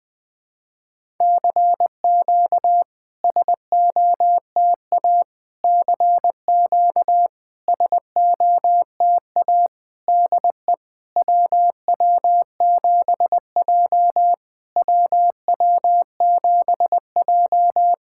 Morse code, heard as a keyed tone, about 20 wpm, 700 Hz. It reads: CQ SOTA CQ SOTA DE WW7J WW7J
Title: SOTA QSO Practice 20wpm